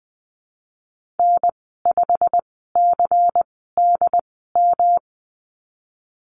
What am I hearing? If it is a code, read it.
Morse code, heard as a keyed tone, about 20 wpm, spaced as for 15 wpm, 700 Hz. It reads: N5CDM
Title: 167 Lesson 27g - Number 7 - Call Signs 20wpm